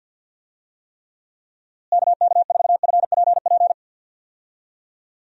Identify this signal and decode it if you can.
Morse code, heard as a keyed tone, about 50 wpm, 700 Hz. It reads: KK4FPP
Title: USA Call Signs - 6 Characters - Repeated 3x - 50wpm